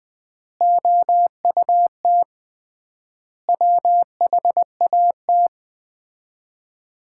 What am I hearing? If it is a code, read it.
Morse code, heard as a keyed tone, about 21 wpm, 700 Hz. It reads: OUT WHAT
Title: 3x Word-Spacing - Sets of 5 Words - 20wpm